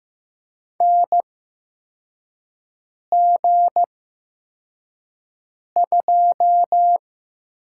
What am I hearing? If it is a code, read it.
Morse code, heard as a keyed tone, about 15 wpm, spaced as for 4 wpm, 700 Hz. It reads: NG2
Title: USA Call Signs - 4 Characters - 8x Character Spacing - 15wpm